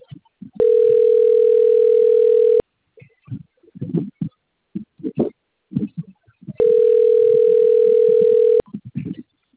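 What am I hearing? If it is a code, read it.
Ringback tone (North America), 440 + 480 Hz.
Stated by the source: Outbound call recorded at a Limeira telemarketing center.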